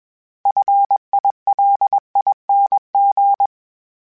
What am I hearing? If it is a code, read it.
Morse code, heard as a keyed tone, about 21 wpm, 800 Hz. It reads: FILING
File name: Special INGs / Special INGs-mp3s / Sngl Ing @20